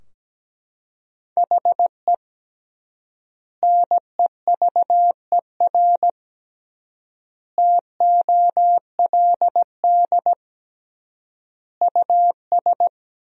Morse code, heard as a keyed tone, about 17 wpm, 700 Hz. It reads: HE NEVER TOLD US
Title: 3x Word-Spacing - Sentences from Top 400 Words - 17wpm